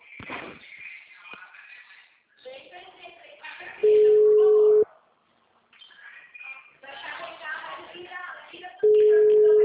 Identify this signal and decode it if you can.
Ringback tone (Europe, 425 Hz), with speech.